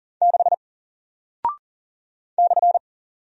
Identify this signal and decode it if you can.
Morse code, heard as a keyed tone, about 39 wpm, 700 Hz. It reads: 6 /
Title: Single Character (Letter:Number:Punctuation:Prosign) 40wpm